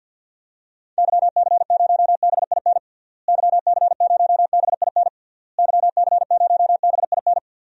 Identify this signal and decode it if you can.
Morse code, heard as a keyed tone, about 52 wpm, 700 Hz. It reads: YC0BIN YC0BIN YC0BIN
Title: DXCC-100 Call Signs - 6 Characters - Repeated 3x - 50wpm